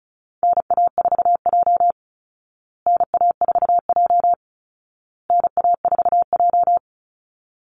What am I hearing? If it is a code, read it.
Morse code, heard as a keyed tone, about 34 wpm, 700 Hz. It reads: NA4J NA4J NA4J
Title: Call Signs Repeated 2x - 35wpm